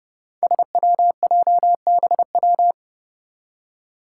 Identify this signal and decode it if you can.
Morse code, heard as a keyed tone, about 30 wpm, 700 Hz. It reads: SWJBW
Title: Five Challenging Letter-Number Combinations 30wpm